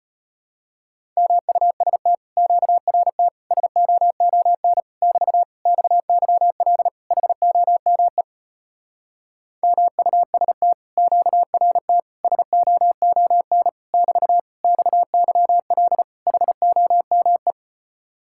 Morse code, heard as a keyed tone, about 33 wpm, 700 Hz. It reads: MUST QRT SOON <BT> XYL HOME MUST QRT SOON <BT> XYL HOME